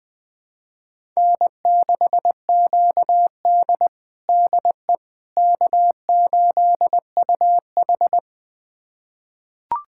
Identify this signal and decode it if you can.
Morse code, heard as a keyed tone, about 20 wpm, 700 Hz. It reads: N6QD DE K8UH
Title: USA Callsign DE Callsign - 4 Characters - 20wpm